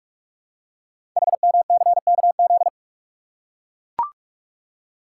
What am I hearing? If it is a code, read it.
Morse code, heard as a keyed tone, about 44 wpm, 700 Hz. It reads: HMCKZ
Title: Code Groups - 5 Letters - 45wpm